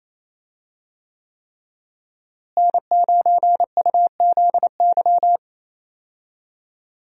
Morse code, heard as a keyed tone, about 27 wpm, 700 Hz. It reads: N9UZY
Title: USA Call Signs - 5 Characters - Repeated 3x - 28wpm